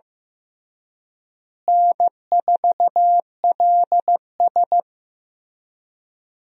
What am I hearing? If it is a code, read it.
Morse code, heard as a keyed tone, about 16 wpm, 700 Hz. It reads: N4LS
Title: USA Call Signs - 4 Characters - Repeated 3x - 15wpm